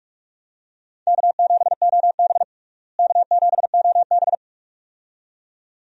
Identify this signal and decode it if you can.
Morse code, heard as a keyed tone, about 46 wpm, 700 Hz. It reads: K7OB K7OB
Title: USA Call Signs - 4 Characters - Repeated 2x - 45wpm